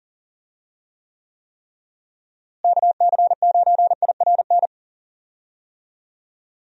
Morse code, heard as a keyed tone, about 37 wpm, 700 Hz. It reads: KC9IRN
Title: USA Call Signs - 6 Characters - Repeated 3x - 40wpm